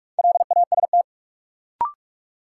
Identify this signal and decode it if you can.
Morse code, heard as a keyed tone, about 46 wpm, 700 Hz. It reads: LAST
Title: Sentences from Top 300 Words - New Word Drill - 45wpm